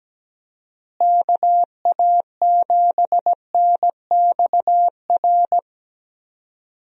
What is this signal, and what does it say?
Morse code, heard as a keyed tone, about 17 wpm, 700 Hz. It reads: KA7NXR
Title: USA Call Signs - 6 Characters - Repeated 2x - 17wpm